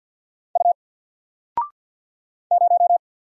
Morse code, heard as a keyed tone, about 48 wpm, 700 Hz. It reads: U 0
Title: Single Letter-Number Alt 50wpm